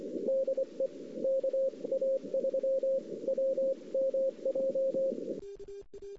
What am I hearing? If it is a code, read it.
Morse code, heard as a keyed tone, about 24 wpm, spaced as for 21 wpm, 530 Hz. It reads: DE KA3WMJ
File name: CW / 2008-ss-cw-k5tr-op-n5rz -06:00